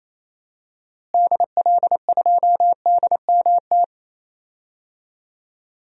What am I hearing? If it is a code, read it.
Morse code, heard as a keyed tone, about 28 wpm, 700 Hz. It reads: DL2DMT
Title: DXCC-100 Call Signs - 6 Characters - Repeated 2x - 28wpm